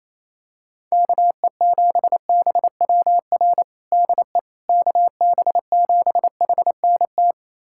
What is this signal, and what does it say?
Morse code, heard as a keyed tone, about 28 wpm, 700 Hz. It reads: KE7BWR DE KB7HNT
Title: USA Callsign DE Callsign - 6 Characters - 28wpm